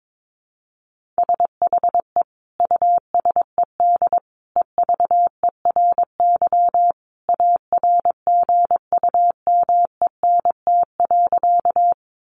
Morse code, heard as a keyed tone, about 22 wpm, 700 Hz. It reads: SHE USED EVERY ARGUMENT.
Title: Sentences from Top 1400 Words 22wpm